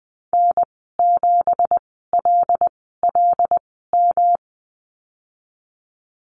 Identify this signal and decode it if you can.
Morse code, heard as a keyed tone, about 20 wpm, spaced as for 15 wpm, 700 Hz. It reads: N7LLM
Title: 167 Lesson 27g - Number 7 - Call Signs 20wpm